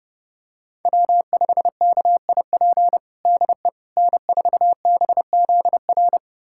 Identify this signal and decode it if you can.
Morse code, heard as a keyed tone, about 30 wpm, 700 Hz. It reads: W5KIP DE N4BZR
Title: USA Callsign DE Callsign - 5 Characters - 30wpm